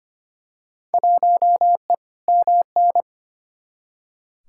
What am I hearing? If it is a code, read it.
Morse code, heard as a keyed tone, about 25 wpm, 700 Hz. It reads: 1E MN
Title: Field Day Exchanges - 25wpm